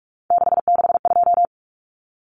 Morse code, heard as a keyed tone, about 44 wpm, 700 Hz. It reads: 662
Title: Common Cut Numbers - 3-Digit 45wpm